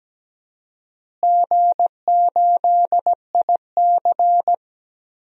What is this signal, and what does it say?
Morse code, heard as a keyed tone, about 17 wpm, 700 Hz. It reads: G8IC
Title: DXCC-100 Call Signs - 4 Characters - Repeated 2x - 17wpm